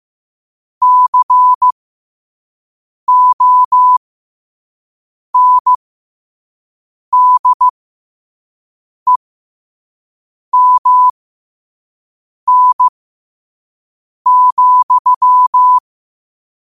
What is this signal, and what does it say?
Morse code, heard as a keyed tone, about 15 wpm, spaced as for 5 wpm, 1000 Hz. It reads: CONDEMN,
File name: qotdfiles / 5wpm